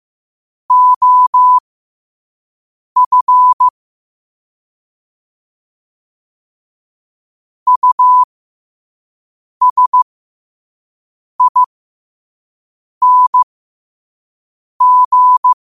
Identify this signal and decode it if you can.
Morse code, heard as a keyed tone, about 15 wpm, spaced as for 5 wpm, 1000 Hz. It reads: OF USING
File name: qotdfiles / 5wpm